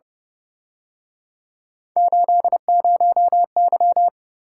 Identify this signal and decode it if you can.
Morse code, heard as a keyed tone, about 30 wpm, 700 Hz. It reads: 80Y